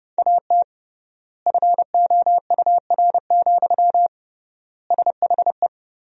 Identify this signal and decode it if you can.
Morse code, heard as a keyed tone, about 30 wpm, 700 Hz. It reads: AT FOUR, SHE
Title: 3x Word-Spacing - Sentences from Top 300 Words - 30wpm